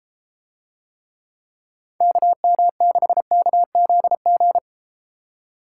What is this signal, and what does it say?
Morse code, heard as a keyed tone, about 33 wpm, 700 Hz. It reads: KM6KZG